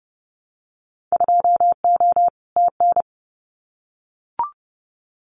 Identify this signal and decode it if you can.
Morse code, heard as a keyed tone, about 30 wpm, 700 Hz. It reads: 2O TN
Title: Winter Field Day Exchanges 30wpm